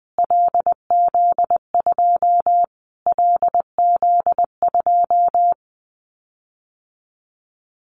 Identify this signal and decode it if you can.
Morse code, heard as a keyed tone, about 20 wpm, 700 Hz. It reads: LZ2 LZ2
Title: Callsign Prefixes for Top 25 DXCC - Repeated 3x - 20wpm